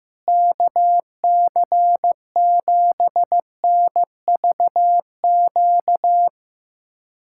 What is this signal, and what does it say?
Morse code, heard as a keyed tone, about 15 wpm, 700 Hz. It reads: KC7NVQ
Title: USA Callsign DE Callsign 15wpm